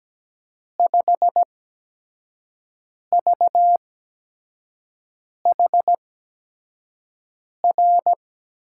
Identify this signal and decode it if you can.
Morse code, heard as a keyed tone, about 17 wpm, spaced as for 5 wpm, 700 Hz. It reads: 5VHR
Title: USA Call Signs - 6 Characters - 8x Character Spacing - 17wpm